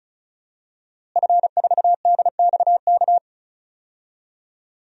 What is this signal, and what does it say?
Morse code, heard as a keyed tone, about 34 wpm, 700 Hz. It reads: F4DXK
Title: DXCC-100 Call Signs - 5 Characters - Repeated 2x - 35wpm